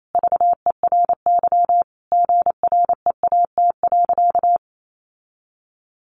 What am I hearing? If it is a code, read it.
Morse code, heard as a keyed tone, about 28 wpm, 700 Hz. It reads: VERY GREAT.
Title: Sentences from Top 1400 Words 28wpm